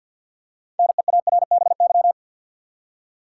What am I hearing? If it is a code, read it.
Morse code, heard as a keyed tone, about 50 wpm, 700 Hz. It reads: NEARBY
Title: Six Letter Words - 50wpm